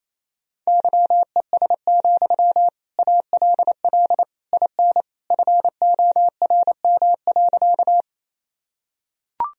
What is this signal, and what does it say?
Morse code, heard as a keyed tone, about 28 wpm, 700 Hz. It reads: YES, ALL IN FORM.
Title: Sentences from Top 1000 Words - Review Part 1 - 28wpm